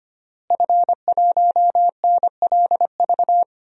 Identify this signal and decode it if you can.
Morse code, heard as a keyed tone, about 25 wpm, 700 Hz. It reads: F1NLV
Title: DXCC-100 Call Signs - 5 Characters - Repeated 3x - 25wpm